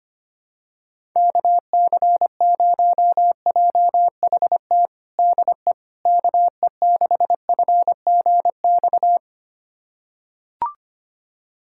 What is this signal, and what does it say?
Morse code, heard as a keyed tone, about 25 wpm, 700 Hz. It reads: KC0JHT DE KE6FGX
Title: USA Callsign DE Callsign - 6 Characters - 25wpm